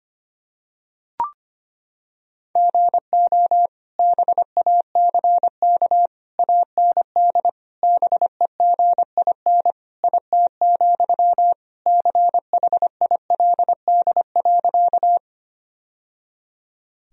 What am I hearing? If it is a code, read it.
Morse code, heard as a keyed tone, about 25 wpm, 700 Hz. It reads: GO BACK AND BEGIN IT, CHILD.
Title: Sentences from Top 1000 Words - Review Part 1 - 25wpm